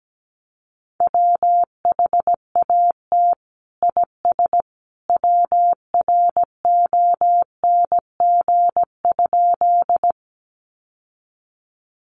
Morse code, heard as a keyed tone, about 17 wpm, 700 Hz. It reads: WHAT IS WRONG?
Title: Sentences from Top 900 Words 17wpm